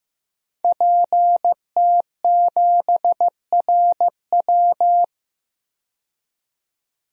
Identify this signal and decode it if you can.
Morse code, heard as a keyed tone, about 15 wpm, 700 Hz. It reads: PT7RW
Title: DXCC-100 Call Signs - 5 Characters - 15wpm